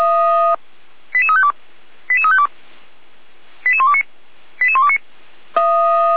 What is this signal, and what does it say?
Five-tone selective call (ZVEI), heard as quick runs of five tones, tones of about 70 ms. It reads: 88352 88352 88129 88129
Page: SDR/ZVEI/88288